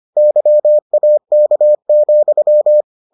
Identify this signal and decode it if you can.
Morse code, heard as a keyed tone, about 25 wpm, 600 Hz. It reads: YAK,